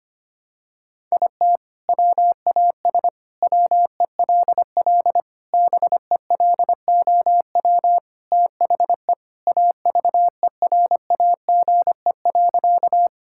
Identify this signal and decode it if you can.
Morse code, heard as a keyed tone, about 25 wpm, 700 Hz. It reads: IT WAS WELL BELOW THE AVERAGE.
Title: Sentences from Top 1000 Words - Review Part 6 - 25wpm